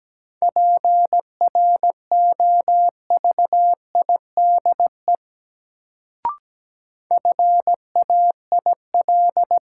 Morse code, heard as a keyed tone, about 17 wpm, 700 Hz. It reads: PROVIDE FAIL
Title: Sentences from Top 1100 Words - New Word Drill Alt - 17wpm